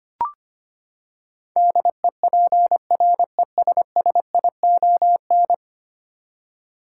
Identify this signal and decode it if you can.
Morse code, heard as a keyed tone, about 25 wpm, 700 Hz. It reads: DEPRESSION
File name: Ten Letter Words - 25wpm